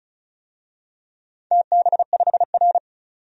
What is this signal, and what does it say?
Morse code, heard as a keyed tone, about 35 wpm, 700 Hz. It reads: TB5R
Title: DXCC-100 Call Signs - 4 Characters - 35wpm